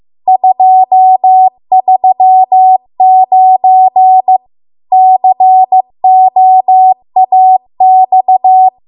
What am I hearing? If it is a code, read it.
Morse code, heard as a keyed tone, about 15 wpm, 750 Hz. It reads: 239 COAX
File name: Archive / 15 WPM / 15WPM